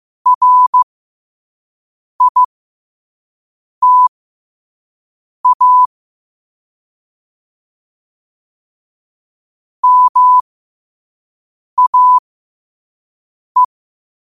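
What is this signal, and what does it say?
Morse code, heard as a keyed tone, about 15 wpm, spaced as for 5 wpm, 1000 Hz. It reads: RITA MAE
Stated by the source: Morse code quote practice at 5 words per minute.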